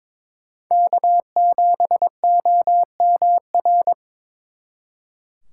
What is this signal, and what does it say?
Morse code, heard as a keyed tone, about 25 wpm, 700 Hz. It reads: K7OMR
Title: Speed Racing - USA 1x3 Call Signs 22wpm